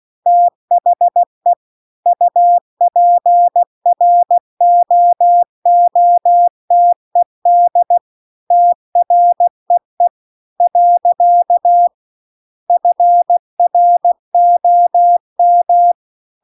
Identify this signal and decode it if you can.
Morse code, heard as a keyed tone, about 16 wpm, 700 Hz. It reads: THE UPROOTED TREE . FROM
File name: CW-The-Big-Twoheaded-River-2-16WPM